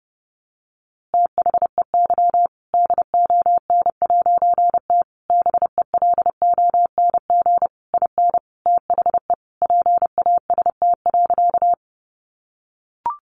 Morse code, heard as a keyed tone, about 30 wpm, 700 Hz. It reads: THEY DON'T BELONG IN THE PAST.
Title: A Sound Of Thunder 30wpm